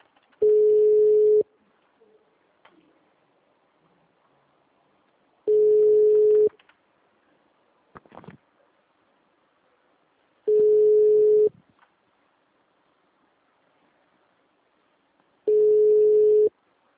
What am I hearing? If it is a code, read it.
Ringback tone (Europe, 425 Hz).